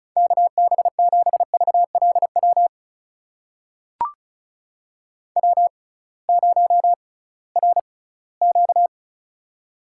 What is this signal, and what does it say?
Morse code, heard as a keyed tone, about 36 wpm, spaced as for 20 wpm, 700 Hz. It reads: KB7VLW W0RQ
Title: USA Call Signs - 6x Character Spacing - 35wpm